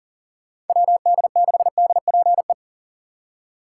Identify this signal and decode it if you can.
Morse code, heard as a keyed tone, about 50 wpm, 700 Hz. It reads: WD6DPE
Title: Speed Racing - USA 2x3 Call Signs 40wpm